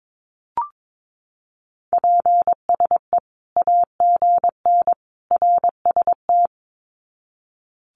Morse code, heard as a keyed tone, about 22 wpm, 700 Hz. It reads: PSE AGN RST
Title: Common QSO Phrases 22wpm